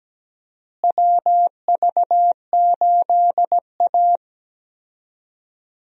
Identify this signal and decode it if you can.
Morse code, heard as a keyed tone, about 17 wpm, 700 Hz. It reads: WV8A
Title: USA Call Signs - 4 Characters - Repeated 3x - 17wpm